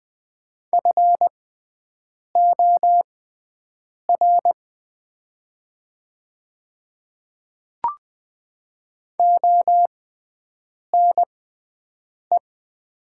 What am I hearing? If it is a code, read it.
Morse code, heard as a keyed tone, about 20 wpm, spaced as for 7 wpm, 690 Hz. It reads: FOR ONE
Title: 6x Character Spacing - Top 100 Words - 20wpm